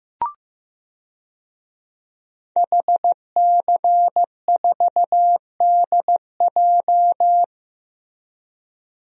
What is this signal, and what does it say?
Morse code, heard as a keyed tone, about 16 wpm, 700 Hz. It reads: HC4DJ
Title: DXCC-100 Call Signs - 5 Characters - Repeated 3x - 15wpm